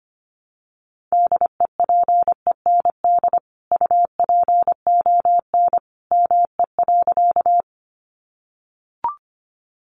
Morse code, heard as a keyed tone, about 25 wpm, 700 Hz. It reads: DEPEND UPON ME.
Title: Sentences from Top 1300 Words 25wpm